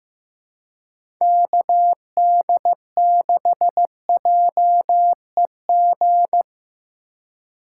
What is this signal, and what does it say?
Morse code, heard as a keyed tone, about 15 wpm, 700 Hz. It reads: KD6JEG
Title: USA Call Signs - 6 Characters - 4x Character Spacing - 15wpm